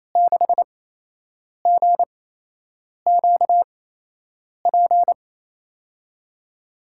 Morse code, heard as a keyed tone, about 28 wpm, spaced as for 8 wpm, 700 Hz. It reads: 6GQP
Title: USA Call Signs - 6 Characters - 8x Character Spacing - 28wpm